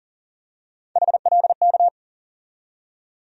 Morse code, heard as a keyed tone, about 38 wpm, 700 Hz. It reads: HLK
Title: USA Call Sign Postfixes - 3 Letters 40wpm